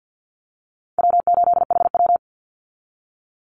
Morse code, heard as a keyed tone, about 50 wpm, 700 Hz. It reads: W8HW
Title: Easy Call Signs Repeated 3x - 50wpm